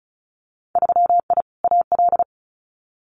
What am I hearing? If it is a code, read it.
Morse code, heard as a keyed tone, about 36 wpm, 700 Hz. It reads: 3I AL
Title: Winter Field Day Exchanges Encore 35wpm